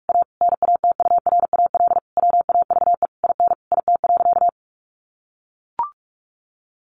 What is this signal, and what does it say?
Morse code, heard as a keyed tone, about 44 wpm, 700 Hz. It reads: A NATURAL WAVE IN IT.
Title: Sentences from Top 1400 Words 45wpm